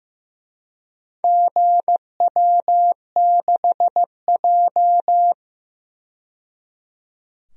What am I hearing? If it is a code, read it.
Morse code, heard as a keyed tone, about 15 wpm, 700 Hz. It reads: GW6J
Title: DXCC-100 Call Signs - 4 Characters - 15wpm